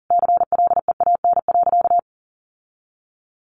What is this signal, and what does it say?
Morse code, heard as a keyed tone, about 40 wpm, 700 Hz. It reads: CLEAN.
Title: Sentences from Top 1100 Words 40wpm